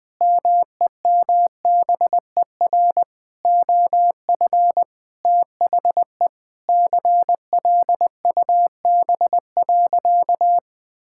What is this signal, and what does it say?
Morse code, heard as a keyed tone, about 20 wpm, 700 Hz. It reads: MEMBER OF THE CLUB.
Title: Sentences with 5-Word Phrases - Spoken Beginning - 20wpm